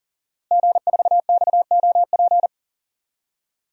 Morse code, heard as a keyed tone, about 40 wpm, 700 Hz. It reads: G4XOP
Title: DXCC-100 Call Signs - 5 Characters - Repeated 2x - 40wpm